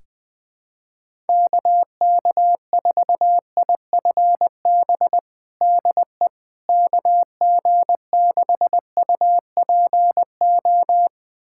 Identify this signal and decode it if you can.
Morse code, heard as a keyed tone, about 20 wpm, 700 Hz. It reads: KK4IFB DE KG6UPO
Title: USA Callsign DE Callsign - 6 Characters - 20wpm